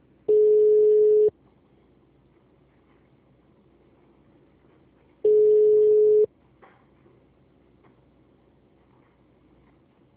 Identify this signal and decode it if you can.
Ringback tone (Europe, 425 Hz).
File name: out- -2006- - - .44